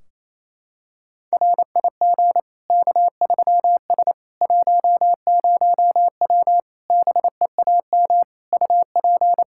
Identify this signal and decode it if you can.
Morse code, heard as a keyed tone, about 27 wpm, 700 Hz. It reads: RIG K3S 10W BEAM UP
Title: Practice QSO - Exchange 2 - Info - 28wpm